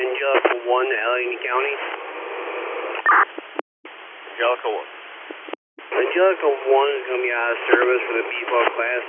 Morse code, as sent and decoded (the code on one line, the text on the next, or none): none